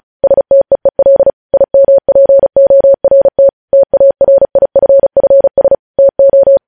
Morse code, heard as a keyed tone, about 36 wpm, 550 Hz. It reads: STEEL IMPORT TARIFFS TO
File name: podcast / News headlines-1x-35x35